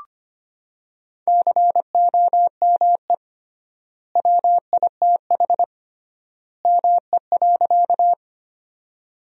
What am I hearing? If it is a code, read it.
Morse code, heard as a keyed tone, about 25 wpm, 700 Hz. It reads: COME WITH ME.
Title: 3x Word-Spacing - Sentences from Top 200 Words - 25wpm